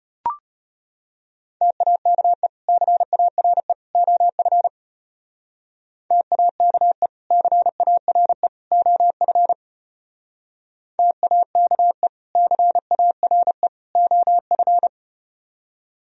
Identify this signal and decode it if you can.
Morse code, heard as a keyed tone, about 33 wpm, 700 Hz. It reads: TAKE CARE OF TAKE CARE OF TAKE CARE OF